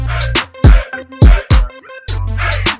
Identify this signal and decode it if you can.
There is no code.